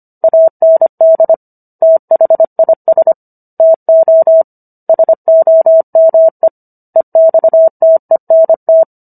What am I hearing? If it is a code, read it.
Morse code, heard as a keyed tone, about 25 wpm, 650 Hz. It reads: AND THIS TO SOME EXTENT
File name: Book One / 16-Chapter 16 - 25 WPM